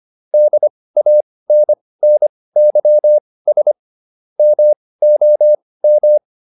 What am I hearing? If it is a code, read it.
Morse code, heard as a keyed tone, about 25 wpm, spaced as for 18 wpm, 600 Hz. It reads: DANNYS MOM